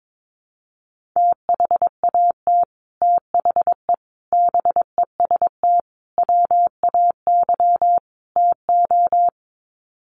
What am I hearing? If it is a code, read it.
Morse code, heard as a keyed tone, about 22 wpm, 700 Hz. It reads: THAT THE BEST WAY TO
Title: Sets of 5 Words Encore2 22wpm